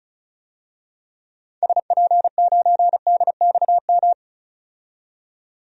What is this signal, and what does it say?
Morse code, heard as a keyed tone, about 36 wpm, 700 Hz. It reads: SP9DXM
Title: DXCC-100 Call Signs - 6 Characters - 35wpm